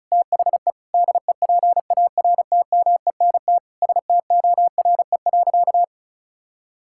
Morse code, heard as a keyed tone, about 36 wpm, 700 Hz. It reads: THE DEPARTMENT STORE.
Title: Sentences from Top 1000 Words - Review Part 5 - 35wpm